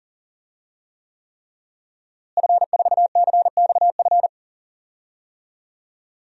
Morse code, heard as a keyed tone, about 37 wpm, 700 Hz. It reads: F4CXF
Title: DXCC-100 Call Signs - 5 Characters - Repeated 3x - 40wpm